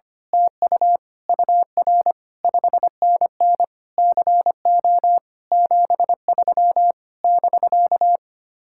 Morse code, heard as a keyed tone, about 24 wpm, 700 Hz. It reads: TU UR 5NN CO 73 <BK>
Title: Speed-Racing Common QSO Phrases - 25wpm